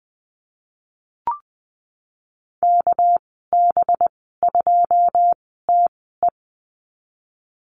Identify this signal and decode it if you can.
Morse code, heard as a keyed tone, about 20 wpm, spaced as for 15 wpm, 700 Hz. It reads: KB2TE
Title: 207 Lesson 33g - Letter j - Call Signs 20wpm